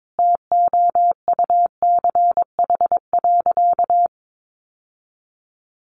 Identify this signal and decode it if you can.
Morse code, heard as a keyed tone, about 22 wpm, 700 Hz. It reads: TOUCH.
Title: Sentences from Top 1500 Words 22wpm